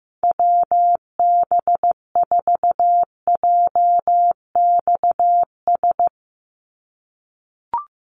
Morse code, heard as a keyed tone, about 15 wpm, 700 Hz. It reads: WB4JXS
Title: USA Call Signs 15wpm